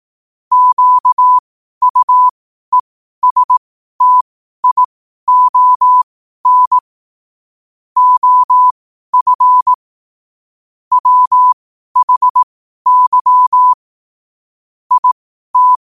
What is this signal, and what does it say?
Morse code, heard as a keyed tone, about 18 wpm, spaced as for 12 wpm, 1000 Hz. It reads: QUESTION OF WHY IT
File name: qotdfiles / 12wpm